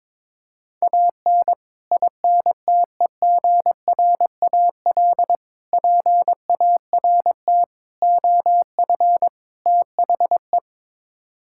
Morse code, heard as a keyed tone, about 22 wpm, 700 Hz. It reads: AN INTEGRAL PART OF THE
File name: practice / 22wpm / Build A Phrase - 5-Word Phrases - 22wpm